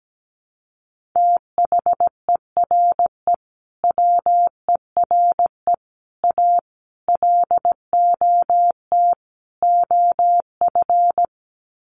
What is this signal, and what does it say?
Morse code, heard as a keyed tone, about 17 wpm, 700 Hz. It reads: THERE WERE A LOT OF
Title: Sets of 5 Words 17wpm